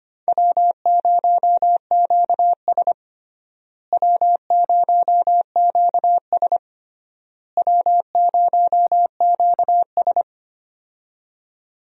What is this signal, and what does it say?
Morse code, heard as a keyed tone, about 25 wpm, 700 Hz. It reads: W0QS W0QS W0QS
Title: USA Call Signs - 4 Characters - Repeated 3x - 25wpm